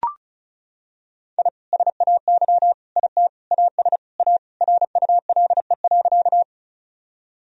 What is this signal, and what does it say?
Morse code, heard as a keyed tone, about 36 wpm, 700 Hz. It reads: I SAY IT AS A RULE.
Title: Sentences from Top 1000 Words - Review Part 2 - 35wpm